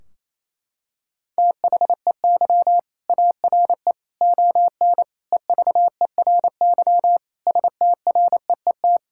Morse code, heard as a keyed tone, about 28 wpm, 700 Hz. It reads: THEY ARE ON EVERY STREET
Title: Sentences from Top 1600 Words 28wpm